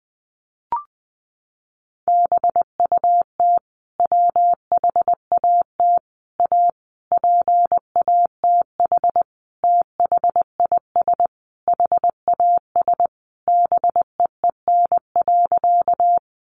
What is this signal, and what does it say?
Morse code, heard as a keyed tone, about 20 wpm, 700 Hz. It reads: BUT WHAT A PATH THIS HAS BEEN.
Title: Sentences from Top 1400 Words 20wpm